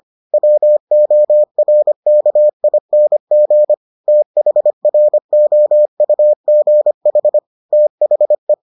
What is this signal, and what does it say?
Morse code, heard as a keyed tone, about 25 wpm, 600 Hz. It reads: WORKING THROUGH THE